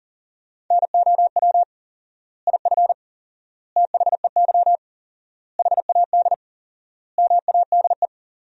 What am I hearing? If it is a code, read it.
Morse code, heard as a keyed tone, about 40 wpm, 700 Hz. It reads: NOW IF THEY HAD MADE
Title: 4x Word-Spacing - Sentences from Top 500 Words - 40wpm